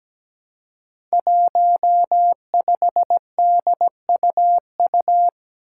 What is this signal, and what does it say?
Morse code, heard as a keyed tone, about 17 wpm, 700 Hz. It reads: 15DUU
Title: Five Challenging Letter-Number Combinations 17wpm